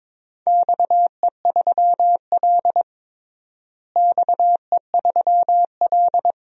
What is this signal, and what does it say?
Morse code, heard as a keyed tone, about 22 wpm, 700 Hz. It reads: XE3L XE3L
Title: DXCC-100 Call Signs - 4 Characters - Repeated 3x - 22wpm